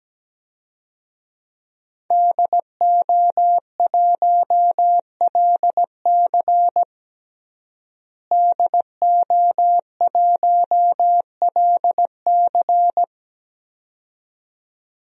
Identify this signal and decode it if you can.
Morse code, heard as a keyed tone, about 17 wpm, 700 Hz. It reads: DO1LC DO1LC
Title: DXCC-100 Call Signs - 5 Characters - Repeated 2x - 17wpm